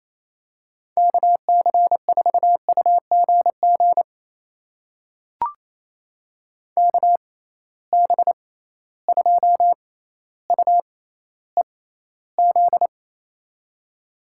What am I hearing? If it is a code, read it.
Morse code, heard as a keyed tone, about 28 wpm, spaced as for 14 wpm, 700 Hz. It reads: KC4UGG KB2UEZ
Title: USA Call Signs - 6 Characters - 6x Character Spacing - 28wpm